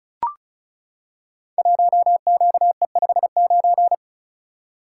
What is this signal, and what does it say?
Morse code, heard as a keyed tone, about 36 wpm, 700 Hz. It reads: 1QE59
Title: Code Groups - 5 Letter:Numbers - 35wpm